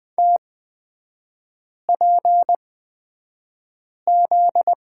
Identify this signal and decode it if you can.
Morse code, heard as a keyed tone, about 20 wpm, spaced as for 5 wpm, 700 Hz. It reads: TPZ
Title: 255 Lesson 41e - Period - Three Characters 20wpm